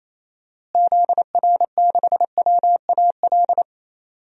Morse code, heard as a keyed tone, about 28 wpm, 700 Hz. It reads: ZR6WAL
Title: DXCC-100 Call Signs - 6 Characters - Repeated 3x - 28wpm